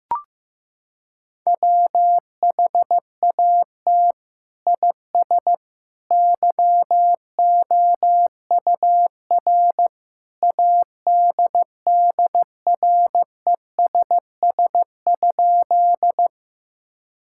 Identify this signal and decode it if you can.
Morse code, heard as a keyed tone, about 15 wpm, 700 Hz. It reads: WHAT IS YOUR ADDRESS?
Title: Sentences from Top 1000 Words - Review Part 6 - 15wpm